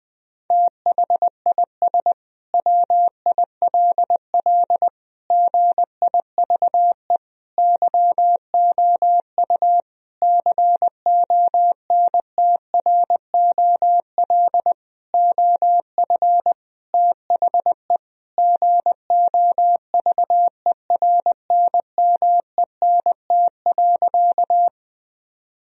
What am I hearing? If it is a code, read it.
Morse code, heard as a keyed tone, about 20 wpm, 700 Hz. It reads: THIS WILL GIVE YOU CONTROL OF THE GOVERNMENT.
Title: Sentences from Top 1000 Words - Review Part 3 - 20wpm